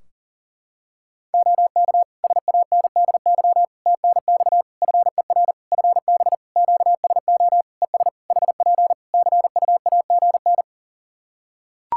Morse code, heard as a keyed tone, about 37 wpm, 700 Hz. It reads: OK SANDY TNX FER FB QSO ES HP CUAGN
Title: K1USN SST QSO Practice 40wpm